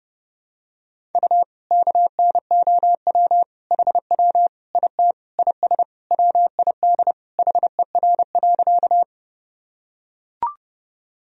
Morse code, heard as a keyed tone, about 30 wpm, 700 Hz. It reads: U KNOW HW IT IS WID HER.
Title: Sentences from Top 100 Words and Abbreviations 30wpm